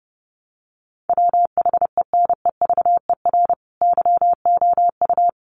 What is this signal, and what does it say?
Morse code, heard as a keyed tone, about 28 wpm, 700 Hz. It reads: WHENEVER YOU
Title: Sentences from Top 1200 Words 30wpm